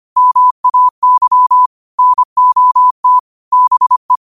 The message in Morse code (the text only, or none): MAY NOT BE